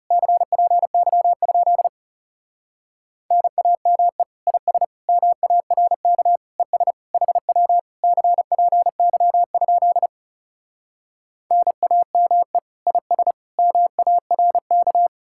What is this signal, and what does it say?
Morse code, heard as a keyed tone, about 40 wpm, 700 Hz. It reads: CPY? NAME IS MARK ES HW CPY? NAME IS MARK